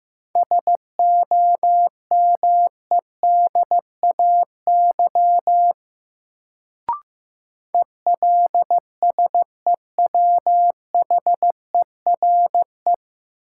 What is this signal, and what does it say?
Morse code, heard as a keyed tone, about 15 wpm, 700 Hz. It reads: SOMEDAY ELSEWHERE
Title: Sentences from Top 1600 Words - New Word Drill Alt - 15wpm